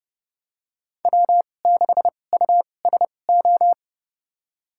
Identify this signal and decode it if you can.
Morse code, heard as a keyed tone, about 30 wpm, spaced as for 22 wpm, 700 Hz. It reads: W6USO
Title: USA Call Signs - 5 Characters - 2x Character Spacing - 30wpm